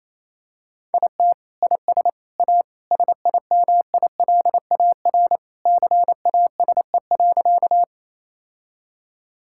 Morse code, heard as a keyed tone, about 28 wpm, 700 Hz. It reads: IT IS A SIMILAR CASE.
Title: Sentences from Top 1000 Words - Review Part 5 - 28wpm